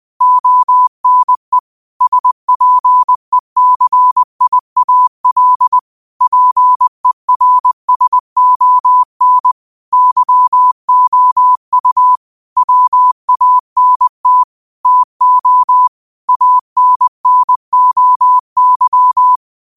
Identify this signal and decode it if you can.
Morse code, heard as a keyed tone, about 20 wpm, 1000 Hz. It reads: ONE SPECIAL PERSON YOU WANT TO ANNOY
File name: qotdfiles / 20wpm